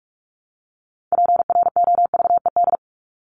Morse code, heard as a keyed tone, about 44 wpm, 700 Hz. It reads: PROVED
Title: ED Suffix - Top 100 Words with Base Words 45wpm